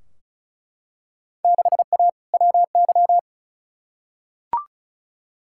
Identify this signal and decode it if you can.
Morse code, heard as a keyed tone, about 36 wpm, 700 Hz. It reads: 6A WY
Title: Field Day Exchanges Encore - 35wpm